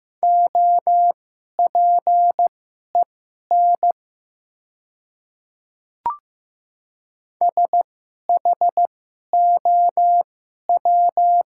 Morse code, heard as a keyed tone, about 15 wpm, spaced as for 11 wpm, 700 Hz. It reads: OPEN SHOW
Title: 2x Character Spacing - Top 300 Words - 15wpm